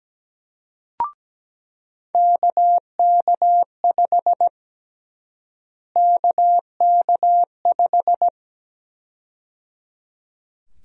Morse code, heard as a keyed tone, about 17 wpm, 700 Hz. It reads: KK5 KK5
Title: USA Call Sign Prefixes - Repeated 2x - 17wpm